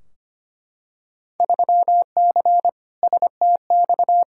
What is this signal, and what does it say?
Morse code, heard as a keyed tone, about 25 wpm, 700 Hz. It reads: 3C STX
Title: Field Day Exchanges Encore - 25wpm